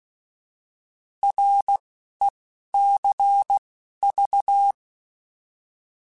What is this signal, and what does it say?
Morse code, heard as a keyed tone, about 16 wpm, 800 Hz. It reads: R E C V
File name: Abreviations / Abreviations mp3s / Abreviations@10